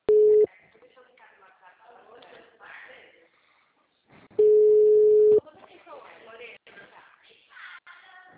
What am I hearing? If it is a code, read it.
Ringback tone (Europe, 425 Hz), with speech.
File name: out- -2011- - - .9187